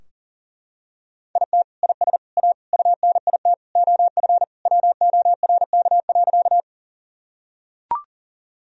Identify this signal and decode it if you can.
Morse code, heard as a keyed tone, about 37 wpm, 700 Hz. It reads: IT IS A UNIT OF WORK.
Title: Sentences from Top 1000 Words - Review Part 2 - 40wpm